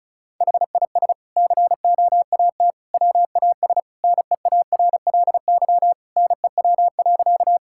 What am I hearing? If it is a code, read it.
Morse code, heard as a keyed tone, about 36 wpm, 700 Hz. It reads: HIS COAT WAS NEARLY NEW.
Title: Sentences from Top 1600 Words 35wpm